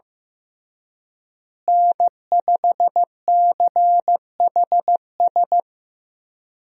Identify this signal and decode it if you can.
Morse code, heard as a keyed tone, about 15 wpm, 700 Hz. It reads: N5CHS
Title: USA Call Signs - 5 Characters - Repeated 3x - 15wpm